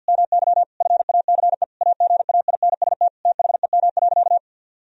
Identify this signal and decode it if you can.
Morse code, heard as a keyed tone, about 51 wpm, 700 Hz. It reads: MY FACE AGAINST THEM.
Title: Sentences from Top 1000 Words - Review Part 2 - 50wpm